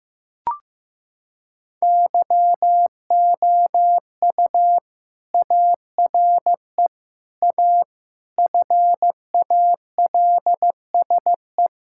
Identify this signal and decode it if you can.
Morse code, heard as a keyed tone, about 15 wpm, 700 Hz. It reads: YOU ARE A FALSE
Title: Sentences from Top 1600 Words 15wpm